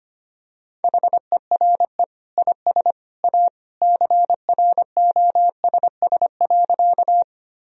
Morse code, heard as a keyed tone, about 25 wpm, 700 Hz. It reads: HERE IS A CROSS.
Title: Sentences from Top 1000 Words - Review Part 1 - 25wpm